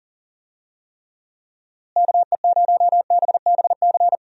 Morse code, heard as a keyed tone, about 40 wpm, 700 Hz. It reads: KE0BBC
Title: USA Call Signs - 6 Characters - Repeated 2x - 40wpm